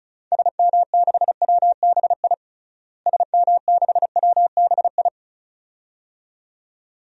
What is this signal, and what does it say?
Morse code, heard as a keyed tone, about 34 wpm, 700 Hz. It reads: SM6WBI SM6WBI
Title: DXCC-100 Call Signs - 6 Characters - Repeated 2x - 35wpm